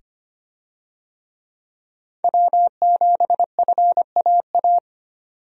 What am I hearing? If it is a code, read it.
Morse code, heard as a keyed tone, about 25 wpm, 700 Hz. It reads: W7FAA
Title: USA Call Signs - 5 Characters - 25wpm